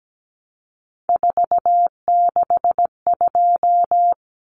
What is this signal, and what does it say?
Morse code, heard as a keyed tone, about 17 wpm, 700 Hz. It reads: 462
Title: Common Cut Numbers - 3-Digit 17wpm